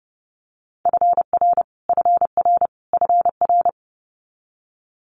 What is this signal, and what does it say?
Morse code, heard as a keyed tone, about 30 wpm, 700 Hz. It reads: FR FR FR
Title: Multi-letter Phonemes Repeated 3x 30wpm